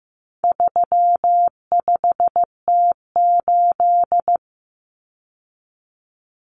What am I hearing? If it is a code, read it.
Morse code, heard as a keyed tone, about 15 wpm, 700 Hz. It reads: 35T8
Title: Common Cut Numbers - 4-Digit 15wpm